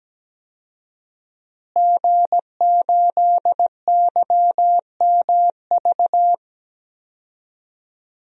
Morse code, heard as a keyed tone, about 17 wpm, 700 Hz. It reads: G8YMV
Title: DXCC-100 Call Signs - 5 Characters - Repeated 2x - 17wpm